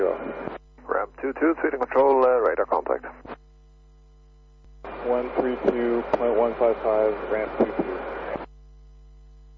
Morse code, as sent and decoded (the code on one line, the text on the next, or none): none